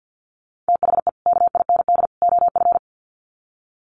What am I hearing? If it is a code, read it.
Morse code, heard as a keyed tone, about 50 wpm, 700 Hz. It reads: THE KIND OF
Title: Sets of 3 Words 50wpm